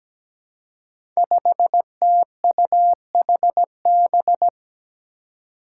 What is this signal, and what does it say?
Morse code, heard as a keyed tone, about 17 wpm, 700 Hz. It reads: 5TUHB
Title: Five Challenging Letter-Number Combinations 17wpm